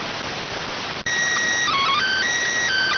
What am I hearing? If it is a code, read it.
SSTV (slow-scan television), PD 290.